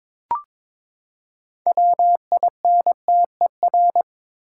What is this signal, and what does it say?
Morse code, heard as a keyed tone, about 23 wpm, 700 Hz. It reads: WINTER
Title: Six Letter Words - 22wpm